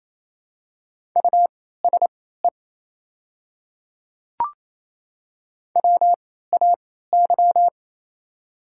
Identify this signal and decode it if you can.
Morse code, heard as a keyed tone, about 27 wpm, spaced as for 16 wpm, 700 Hz. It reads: USE WAY
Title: 3x Character Spacing - Top 100 Words - 28wpm